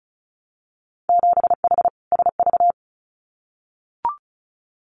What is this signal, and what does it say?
Morse code, heard as a keyed tone, about 36 wpm, 700 Hz. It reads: 7H SV
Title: Winter Field Day Exchanges Encore 35wpm